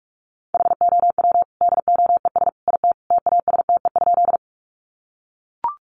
Morse code, heard as a keyed tone, about 44 wpm, 700 Hz. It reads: HOW DOES IT TASTE?
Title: Sentences from Top 1100 Words 45wpm